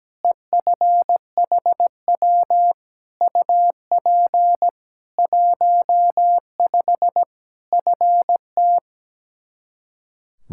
Morse code, heard as a keyed tone, about 17 wpm, 700 Hz. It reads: EFHW UP 15 FT
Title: Practice QSO - Exchange 2 - Info - 17wpm